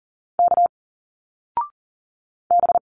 Morse code, heard as a keyed tone, about 40 wpm, 700 Hz. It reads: K B
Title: Single Letters 40wpm